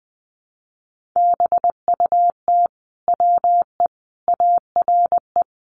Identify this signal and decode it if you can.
Morse code, heard as a keyed tone, about 20 wpm, 700 Hz. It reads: BUT WE ARE
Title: Sentences from Top 800 Words 20wpm